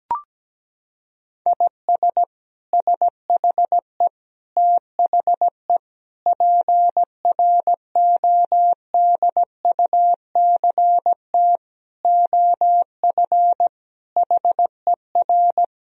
Morse code, heard as a keyed tone, about 17 wpm, 700 Hz. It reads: IS SHE THE PRODUCT OF HER
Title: Sentences from Top 1000 Words - Review Part 2 - 17wpm